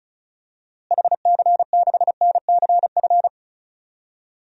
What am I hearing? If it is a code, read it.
Morse code, heard as a keyed tone, about 34 wpm, 700 Hz. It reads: HC6NCF
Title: DXCC-100 Call Signs - 6 Characters - Repeated 3x - 35wpm